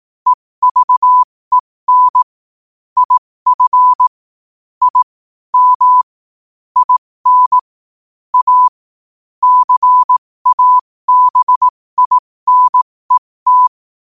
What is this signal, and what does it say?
Morse code, heard as a keyed tone, about 18 wpm, spaced as for 15 wpm, 1000 Hz. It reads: EVEN IF I M IN A CABINET
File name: qotdfiles / 15wpm